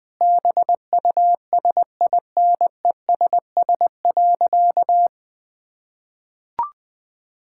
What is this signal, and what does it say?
Morse code, heard as a keyed tone, about 21 wpm, 700 Hz. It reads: BUSINESS.
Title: Sentences from Top 1000 Words - Review Part 6 - 20wpm